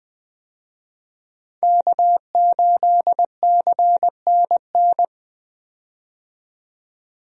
Morse code, heard as a keyed tone, about 20 wpm, 700 Hz. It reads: K8CNN
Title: USA Call Signs - 5 Characters - 20wpm